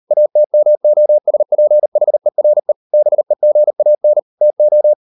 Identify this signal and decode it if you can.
Morse code, heard as a keyed tone, about 38 wpm, 600 Hz. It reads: ATMOSPHERE BEGAN TO